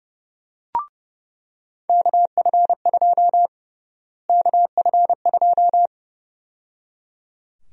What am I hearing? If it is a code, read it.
Morse code, heard as a keyed tone, about 31 wpm, 700 Hz. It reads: KF2 KF2
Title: USA Call Sign Prefixes - Repeated 2x - 30wpm